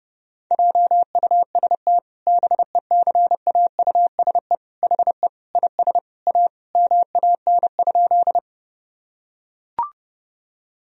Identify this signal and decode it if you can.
Morse code, heard as a keyed tone, about 28 wpm, 700 Hz. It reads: JUST BECAUSE HE IS A MAN?
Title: Sentences from Top 1000 Words - Review Part 3 - 30wpm